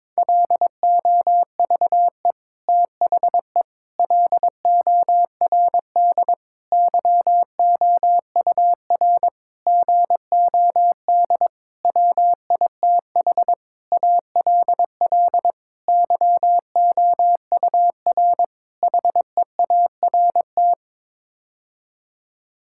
Morse code, heard as a keyed tone, about 22 wpm, 700 Hz. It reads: LOVE THE LORD YOUR GOD WITH ALL YOUR HEART
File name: Sets of 9 Words - 22wpm